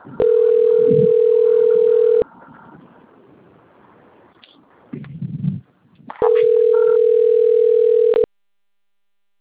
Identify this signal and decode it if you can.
Ringback tone (North America), 440 + 480 Hz.
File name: rg-1600-unknown- - - .26361